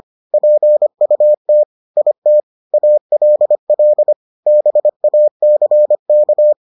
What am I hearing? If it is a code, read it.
Morse code, heard as a keyed tone, about 25 wpm, 600 Hz. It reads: PUT IT ALL BACK